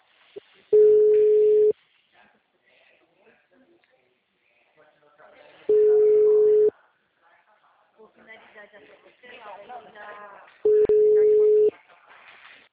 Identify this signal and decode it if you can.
Ringback tone (Europe, 425 Hz), with speech.
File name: out- -2017- - - .46882